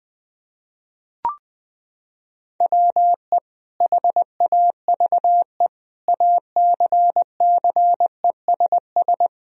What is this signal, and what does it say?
Morse code, heard as a keyed tone, about 20 wpm, 700 Hz. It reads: WE HAVE ACCESS
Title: Sentences from Top 1000 Words - Review Part 4 - 20wpm